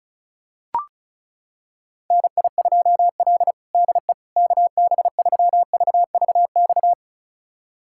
Morse code, heard as a keyed tone, about 34 wpm, 700 Hz. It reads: NI2L DE KB3VVX
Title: USA Callsign DE Callsign 35wpm